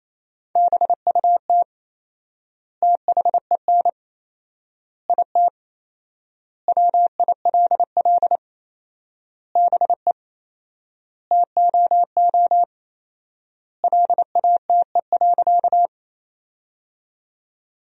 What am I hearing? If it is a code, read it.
Morse code, heard as a keyed tone, about 28 wpm, 700 Hz. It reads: BUT THEN IT WILL BE TOO LATE.
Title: 4x Word-Spacing - Sentences from Top 300 Words - 28wpm